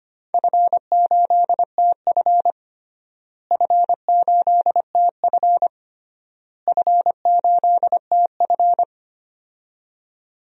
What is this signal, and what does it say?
Morse code, heard as a keyed tone, about 25 wpm, 700 Hz. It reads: F8TF F8TF F8TF
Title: DXCC-100 Call Signs - 4 Characters - Repeated 3x - 25wpm